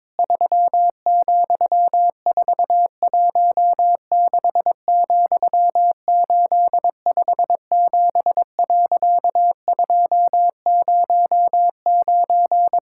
Morse code, heard as a keyed tone, about 22 wpm, 700 Hz. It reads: 3,416,857.209
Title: Pangram Challenge v2 - 22wpm